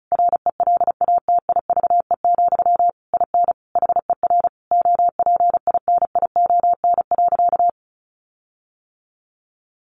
Morse code, heard as a keyed tone, about 34 wpm, 700 Hz. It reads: RELATIVE, IN HER OPINION.
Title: Sentences from Top 1400 Words 35wpm